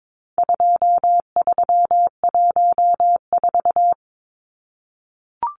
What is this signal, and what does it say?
Morse code, heard as a keyed tone, about 22 wpm, 700 Hz. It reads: 2314
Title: Numbers - 4-Digit 22wpm